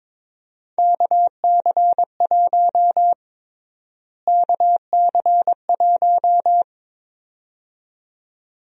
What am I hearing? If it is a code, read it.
Morse code, heard as a keyed tone, about 22 wpm, 700 Hz. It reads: KC1 KC1
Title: USA Call Sign Prefixes - Repeated 2x - 22wpm